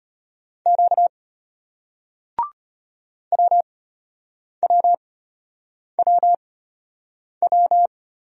Morse code, heard as a keyed tone, about 34 wpm, spaced as for 8 wpm, 700 Hz. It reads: Q WWWW